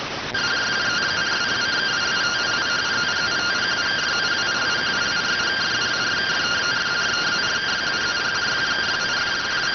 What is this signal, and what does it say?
Two-tone teleprinter signal, about 170 Hz shift.